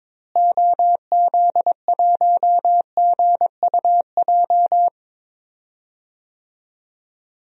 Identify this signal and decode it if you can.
Morse code, heard as a keyed tone, about 21 wpm, 700 Hz. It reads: OZ1GUJ
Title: DXCC-100 Call Signs - 6 Characters - Repeated 3x - 22wpm